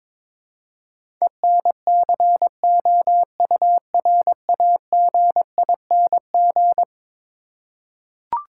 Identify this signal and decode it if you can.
Morse code, heard as a keyed tone, about 22 wpm, 700 Hz. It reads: ENCOURAGING
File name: practice / 22wpm / Eleven Letter Words - 22wpm